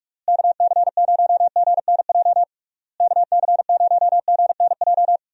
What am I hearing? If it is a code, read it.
Morse code, heard as a keyed tone, about 41 wpm, 700 Hz. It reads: KC0GNJ KC0GNJ
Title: USA Call Signs - 6 Characters - Repeated 3x - 45wpm